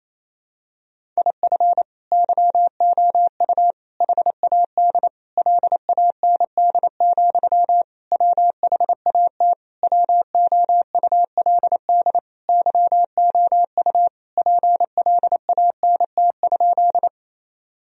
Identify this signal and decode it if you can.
Morse code, heard as a keyed tone, about 27 wpm, 700 Hz. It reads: IF YOU HAD LAND, WHAT WOULD YOU PLANT?
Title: Sentences from Top 1000 Words - Review Part 1 - 28wpm